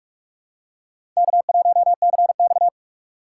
Morse code, heard as a keyed tone, about 41 wpm, 700 Hz. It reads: K1CX
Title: K1USN SST QSO Practice - 45wpm